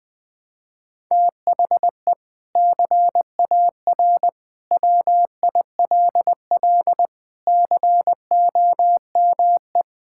Morse code, heard as a keyed tone, about 20 wpm, 700 Hz. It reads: THE CAR WILL COME
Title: Sentences from Top 1600 Words 20wpm